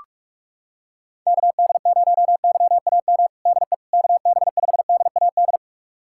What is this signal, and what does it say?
Morse code, heard as a keyed tone, about 47 wpm, 700 Hz. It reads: KD0YAM DE KB5DAD
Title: USA Callsign DE Callsign - 6 Characters - 45wpm